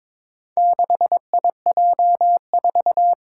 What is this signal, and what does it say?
Morse code, heard as a keyed tone, about 22 wpm, 700 Hz. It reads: 6IJ4
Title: Four Challenging Letter-Number Combinations 22wpm